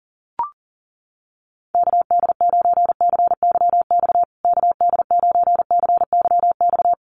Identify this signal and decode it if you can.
Morse code, heard as a keyed tone, about 40 wpm, 700 Hz. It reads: KD9CYX KD9CYX
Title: USA Call Signs Repeated 3x 40wpm